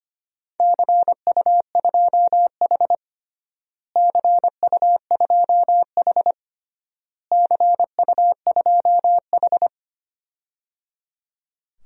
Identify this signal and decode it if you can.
Morse code, heard as a keyed tone, about 25 wpm, 700 Hz. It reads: CU2H CU2H CU2H
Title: DXCC-100 Call Signs - 4 Characters - Repeated 3x - 25wpm